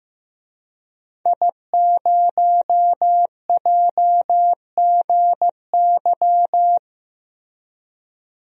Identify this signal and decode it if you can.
Morse code, heard as a keyed tone, about 15 wpm, 700 Hz. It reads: I0JGY
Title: DXCC-100 Call Signs - 5 Characters - Repeated 2x - 15wpm